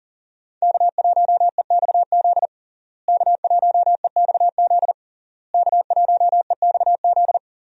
Morse code, heard as a keyed tone, about 40 wpm, 700 Hz. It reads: K1EXZ K1EXZ K1EXZ
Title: USA Call Signs - 5 Characters - Repeated 3x - 40wpm